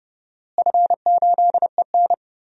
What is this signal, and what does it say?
Morse code, heard as a keyed tone, about 32 wpm, 700 Hz. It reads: F8EN
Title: DXCC-100 Call Signs - 4 Characters - Repeated 2x - 30wpm